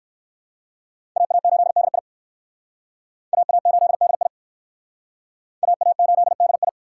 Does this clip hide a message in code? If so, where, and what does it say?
Yes: Morse code from 1.2 to 6.7 s, heard as a keyed tone, about 63 wpm, 700 Hz. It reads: AA8DI AA8DI AA8DI